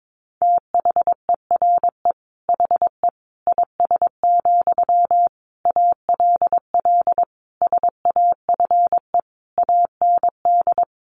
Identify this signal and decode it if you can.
Morse code, heard as a keyed tone, about 22 wpm, 700 Hz. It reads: THERE HE IS, ALL SAFE AND
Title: Sentences from Top 900 Words 22wpm